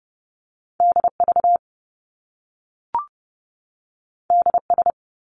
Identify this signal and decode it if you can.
Morse code, heard as a keyed tone, about 30 wpm, 700 Hz. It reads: DV DS
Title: Top Two Letter Combinations in Words 30wpm